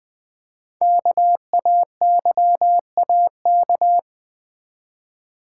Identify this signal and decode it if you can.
Morse code, heard as a keyed tone, about 20 wpm, 700 Hz. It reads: KAYAK
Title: Easy Words with Hints - 20wpm